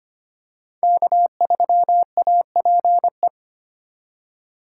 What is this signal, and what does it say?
Morse code, heard as a keyed tone, about 25 wpm, 700 Hz. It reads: K3APE
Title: USA Call Signs - 5 Characters - 2x Character Spacing - 25wpm